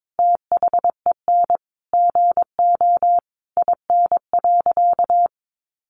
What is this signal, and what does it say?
Morse code, heard as a keyed tone, about 22 wpm, 700 Hz. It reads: THEN GO IN.
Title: Sentences from Top 700 Words 22wpm